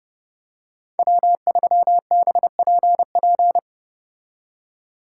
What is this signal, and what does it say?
Morse code, heard as a keyed tone, about 30 wpm, 700 Hz. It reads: W3BPP